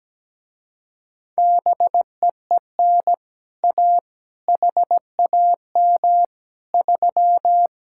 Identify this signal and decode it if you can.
Morse code, heard as a keyed tone, about 17 wpm, 700 Hz. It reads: BEEN A HAM 3
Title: Speed-Racing Common QSO Phrases Encore - 17wpm